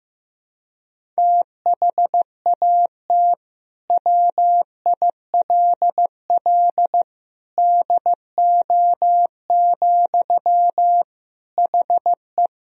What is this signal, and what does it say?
Morse code, heard as a keyed tone, about 15 wpm, 700 Hz. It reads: THAT WILL DO, HE
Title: Sentences from Top 100 Words and Abbreviations 15wpm